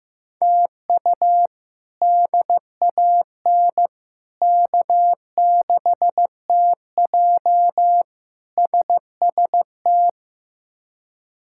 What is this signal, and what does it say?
Morse code, heard as a keyed tone, about 15 wpm, 700 Hz. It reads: TU DAN K6TJ SST
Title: K1USN SST QSO Practice - 15wpm